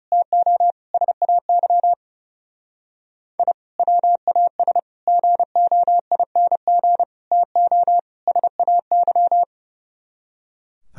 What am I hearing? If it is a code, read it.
Morse code, heard as a keyed tone, about 31 wpm, 700 Hz. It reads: TO SAY I WAS GOING TO SAY
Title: Speed-Racing - Sets of 5 Words 30wpm